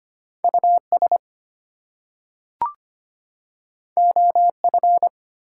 Morse code, heard as a keyed tone, about 25 wpm, 700 Hz. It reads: US OF
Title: Two Letter Words - 25wpm